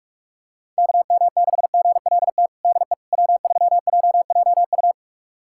Morse code, heard as a keyed tone, about 46 wpm, 700 Hz. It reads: KM6GLT DE W3JJU
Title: USA Callsign DE Callsign 45wpm